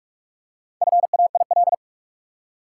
Morse code, heard as a keyed tone, about 46 wpm, 700 Hz. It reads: FAIL
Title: Four Letter Words - 45wpm